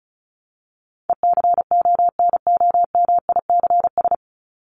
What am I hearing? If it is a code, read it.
Morse code, heard as a keyed tone, about 34 wpm, 700 Hz. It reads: ECONOMICS
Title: Top 501-750 Words 35wpm